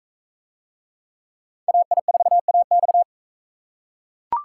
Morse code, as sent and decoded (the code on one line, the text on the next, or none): .- .. ....- .- -..-
AI4AX